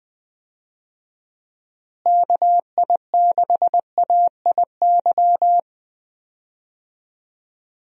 Morse code, heard as a keyed tone, about 20 wpm, 700 Hz. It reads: KI6AIY
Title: USA Call Signs - 6 Characters - Repeated 3x - 20wpm